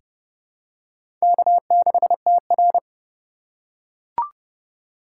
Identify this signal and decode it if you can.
Morse code, heard as a keyed tone, about 30 wpm, 700 Hz. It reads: K6TR
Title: USA Call Signs - 4 Characters - 3x Character Spacing - 30wpm